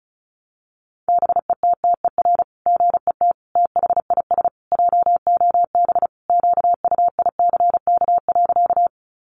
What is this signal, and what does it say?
Morse code, heard as a keyed tone, about 34 wpm, 700 Hz. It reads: BETTER GET THIS JOB QUICK.
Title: Sentences from Top 600 Words 35wpm